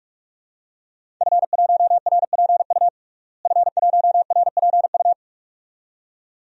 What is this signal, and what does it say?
Morse code, heard as a keyed tone, about 46 wpm, 700 Hz. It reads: F1RPU F1RPU
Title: DXCC-100 Call Signs - 5 Characters - Repeated 2x - 45wpm